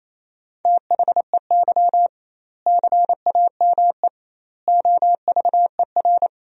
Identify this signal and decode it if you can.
Morse code, heard as a keyed tone, about 27 wpm, 700 Hz. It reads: THEY CAME OVER
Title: 2x Word-Spacing - Sentences from Top 300 Words - 28wpm